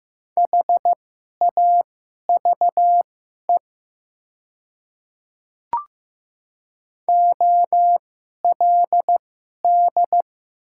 Morse code, heard as a keyed tone, about 15 wpm, spaced as for 11 wpm, 700 Hz. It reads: HAVE OLD
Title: 2x Character Spacing - Top 300 Words - 15wpm